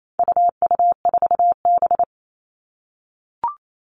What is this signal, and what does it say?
Morse code, heard as a keyed tone, about 28 wpm, 700 Hz. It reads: UU4B
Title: All Cut Numbers - 4-Digit 28wpm